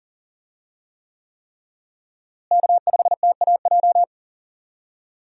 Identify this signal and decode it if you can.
Morse code, heard as a keyed tone, about 40 wpm, 700 Hz. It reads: K5TAJ
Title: USA Call Signs - 5 Characters - Repeated 3x - 40wpm